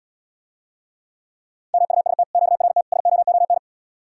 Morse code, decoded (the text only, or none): MUST QRT SOON